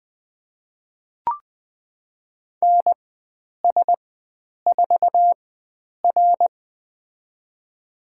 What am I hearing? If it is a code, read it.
Morse code, heard as a keyed tone, about 20 wpm, spaced as for 9 wpm, 700 Hz. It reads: NS4R
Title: USA Call Signs - 4 Characters - 4x Character Spacing - 20wpm